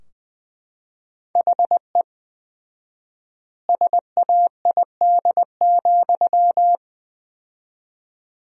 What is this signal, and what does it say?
Morse code, heard as a keyed tone, about 20 wpm, 700 Hz. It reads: HE SAID,
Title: 4x Word-Spacing - Sentences from Top 400 Words - 20wpm